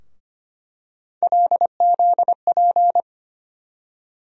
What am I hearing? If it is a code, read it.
Morse code, heard as a keyed tone, about 25 wpm, 700 Hz. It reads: LZP
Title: USA Call Sign Postfixes - 3 Letters 25wpm